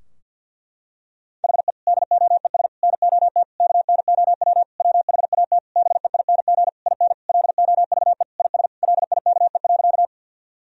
Morse code, heard as a keyed tone, about 49 wpm, 700 Hz. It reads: HE DOES NOT KNOW WHAT BEING IN LOVE IS LIKE.
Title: Sentences from Top 1000 Words - Review Part 3 - 50wpm